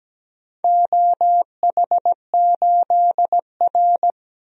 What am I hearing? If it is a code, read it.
Morse code, heard as a keyed tone, about 17 wpm, 700 Hz. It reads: OH8R
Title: DXCC-100 Call Signs - 4 Characters - 17wpm